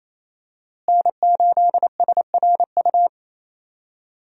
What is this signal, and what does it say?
Morse code, heard as a keyed tone, about 28 wpm, 700 Hz. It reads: N8SRU
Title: USA Call Signs - 5 Characters - Repeated 2x - 28wpm